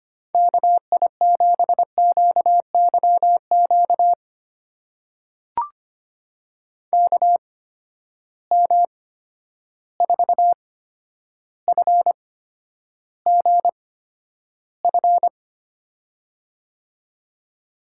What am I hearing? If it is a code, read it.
Morse code, heard as a keyed tone, about 25 wpm, spaced as for 11 wpm, 700 Hz. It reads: KI7QYQ KM4FGF
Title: USA Call Signs - 6 Characters - 8x Character Spacing - 25wpm